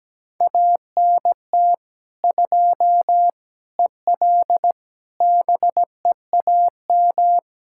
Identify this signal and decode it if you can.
Morse code, heard as a keyed tone, about 17 wpm, 700 Hz. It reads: ANT 2 EL BEAM
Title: Practice QSO - Exchange 2 - Info - 17wpm